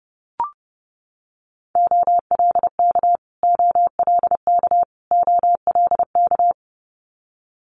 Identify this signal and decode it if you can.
Morse code, heard as a keyed tone, about 30 wpm, 700 Hz. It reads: OLK OLK OLK
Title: Multi-letter Phonemes Repeated 3x 30wpm